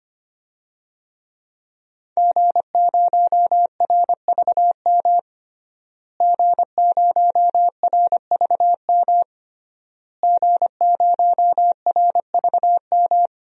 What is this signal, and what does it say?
Morse code, heard as a keyed tone, about 25 wpm, 700 Hz. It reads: G0RVM G0RVM G0RVM
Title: DXCC-100 Call Signs - 5 Characters - Repeated 3x - 25wpm